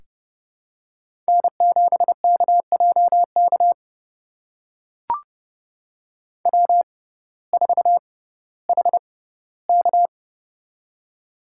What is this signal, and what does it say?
Morse code, heard as a keyed tone, about 30 wpm, spaced as for 17 wpm, 700 Hz. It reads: N7KJK W4HK
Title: USA Call Signs - 6x Character Spacing - 30wpm